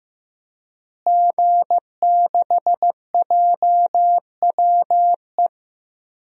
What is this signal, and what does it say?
Morse code, heard as a keyed tone, about 15 wpm, 700 Hz. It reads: G6JWE
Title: DXCC-100 Call Signs - 5 Characters - Repeated 3x - 15wpm